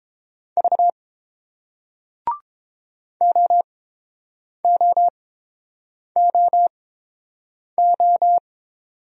Morse code, heard as a keyed tone, about 30 wpm, spaced as for 7 wpm, 700 Hz. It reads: V OOOO